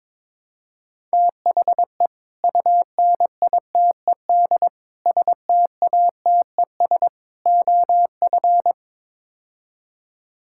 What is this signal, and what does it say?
Morse code, heard as a keyed tone, about 22 wpm, 700 Hz. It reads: THE UNITED STATES OF
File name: practice / 22wpm / Build A Phrase - 5-Word Phrases - 22wpm